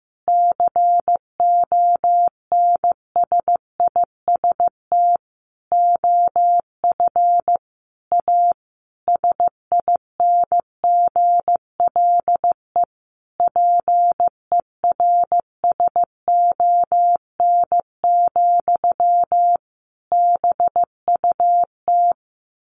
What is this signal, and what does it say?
Morse code, heard as a keyed tone, about 14 wpm, 700 Hz. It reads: CONSIST OF A SINGLE PERSON, BUT
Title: Sentences from Top 1400 Words 15wpm